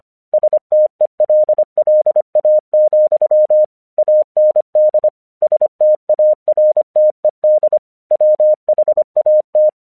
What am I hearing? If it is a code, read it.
Morse code, heard as a keyed tone, about 25 wpm, 610 Hz. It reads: STELLA, AND STARTED WHAT
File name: 1 English 25wpm / AQC10